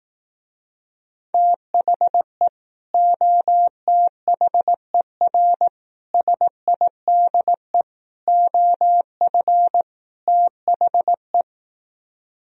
Morse code, heard as a keyed tone, about 18 wpm, 700 Hz. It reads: THE OTHER SIDE OF THE
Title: Speed-Racing - Sets of 5 Words 15wpm